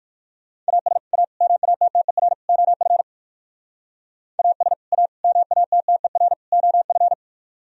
Morse code, heard as a keyed tone, about 56 wpm, 700 Hz. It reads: AS A MATTER OF AS A MATTER OF
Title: Speed-Racing - Sets of 4 Words 45wpm